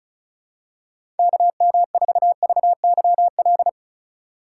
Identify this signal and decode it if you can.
Morse code, heard as a keyed tone, about 34 wpm, 700 Hz. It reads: KM4VYL
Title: USA Call Signs - 6 Characters - 2x Character Spacing - 35wpm